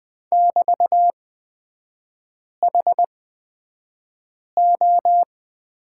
Morse code, heard as a keyed tone, about 20 wpm, spaced as for 5 wpm, 700 Hz. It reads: <BT>HO
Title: 224 Lesson 36e - BT - Three Characters 20wpm